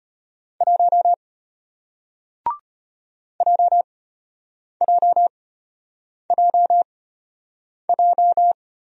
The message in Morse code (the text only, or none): 1 JJJJ